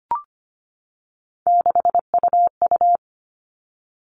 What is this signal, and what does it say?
Morse code, heard as a keyed tone, about 25 wpm, 700 Hz. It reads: 6UU
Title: All Cut Numbers - 3-Digit 25wpm